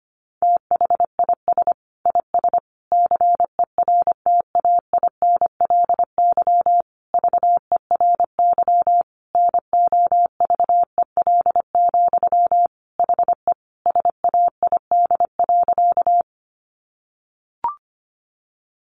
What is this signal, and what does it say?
Morse code, heard as a keyed tone, about 25 wpm, 700 Hz. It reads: THIS IS CERTAINLY VERY NOVEL, HE SAID.
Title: Sentences from Top 1300 Words 25wpm